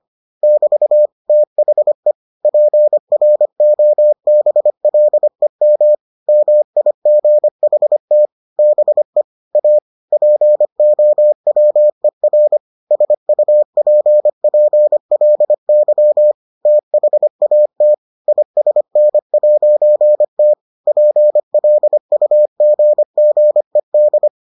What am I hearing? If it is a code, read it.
Morse code, heard as a keyed tone, about 25 wpm, 600 Hz. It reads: <BT> THE PROBLEM MIGHT BE A POWER SUPPLY THAT ISN'T PLUGGED